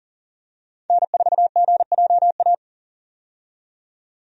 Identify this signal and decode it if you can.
Morse code, heard as a keyed tone, about 40 wpm, 700 Hz. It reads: N4GJA
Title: USA Call Signs - 5 Characters - 40wpm